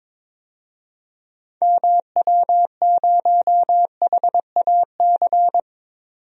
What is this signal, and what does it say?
Morse code, heard as a keyed tone, about 22 wpm, 700 Hz. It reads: MW0HAC
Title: DXCC-100 Call Signs - 6 Characters - 22wpm